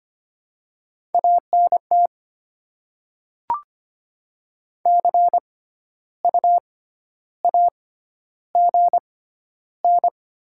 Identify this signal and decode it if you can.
Morse code, heard as a keyed tone, about 25 wpm, spaced as for 12 wpm, 700 Hz. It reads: ANT CU AGN
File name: Top 20 QSO Elements - 6x Character Spacing - 25wpm